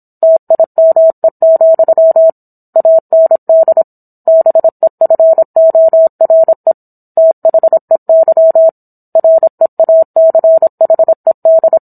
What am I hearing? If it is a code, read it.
Morse code, heard as a keyed tone, about 27 wpm, 650 Hz. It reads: TIME, AND BEFORE THEY REACHED